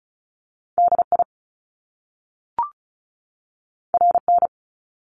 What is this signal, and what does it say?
Morse code, heard as a keyed tone, about 36 wpm, 700 Hz. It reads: DI RN
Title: Top Two Letter Combinations in Words 35wpm